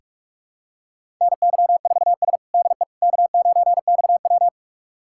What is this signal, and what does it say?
Morse code, heard as a keyed tone, about 46 wpm, 700 Hz. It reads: NY4S DE K9XW
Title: USA Callsign DE Callsign - 4 Characters - 45wpm